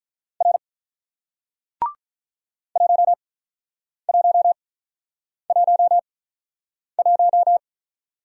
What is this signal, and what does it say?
Morse code, heard as a keyed tone, about 48 wpm, spaced as for 9 wpm, 700 Hz. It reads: R 1111